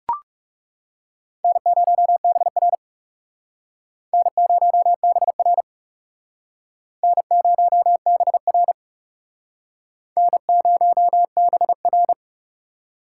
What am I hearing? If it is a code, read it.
Morse code, heard as a keyed tone, about 39 wpm, 700 Hz. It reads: N0BR N0BR N0BR N0BR